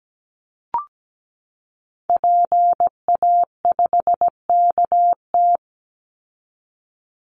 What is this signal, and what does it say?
Morse code, heard as a keyed tone, about 17 wpm, 700 Hz. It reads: PA5KT
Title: Call Signs Repeated 3x - 17wpm